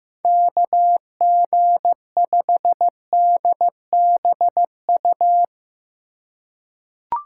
Morse code, heard as a keyed tone, about 15 wpm, 700 Hz. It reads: KG5DBU
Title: USA Call Signs - 6 Characters - 3x Character Spacing - 15wpm